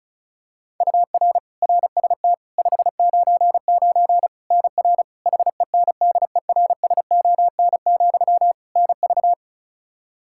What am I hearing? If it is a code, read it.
Morse code, heard as a keyed tone, about 36 wpm, 700 Hz. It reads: UR RST 599 NR HENDERSON, NV
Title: Practice QSO - Exchange 1 - Beginning - 35wpm